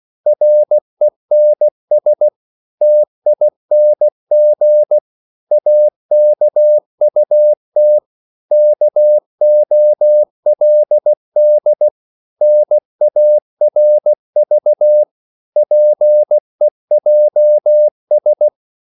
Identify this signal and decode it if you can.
Morse code, heard as a keyed tone, about 16 wpm, 600 Hz. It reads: RENS TING AKUT KOLD NARV PEJS
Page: Korte danske ord 16wpm